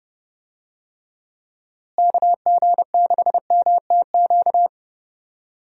Morse code, heard as a keyed tone, about 30 wpm, 700 Hz. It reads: KG6MTQ
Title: USA Call Signs - 6 Characters - Repeated 3x - 30wpm